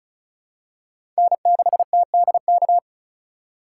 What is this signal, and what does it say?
Morse code, heard as a keyed tone, about 36 wpm, 700 Hz. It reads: N6TDK
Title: USA Call Signs - 5 Characters - 4x Character Spacing - 35wpm